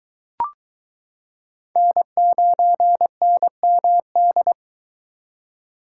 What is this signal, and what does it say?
Morse code, heard as a keyed tone, about 22 wpm, 700 Hz. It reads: N9NMD